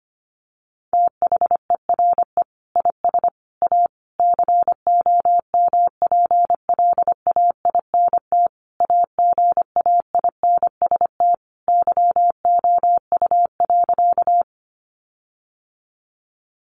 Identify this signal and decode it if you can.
Morse code, heard as a keyed tone, about 25 wpm, 700 Hz. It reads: THERE IS A COMPLAINT AGAINST YOU.
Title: Sentences from Top 1400 Words 25wpm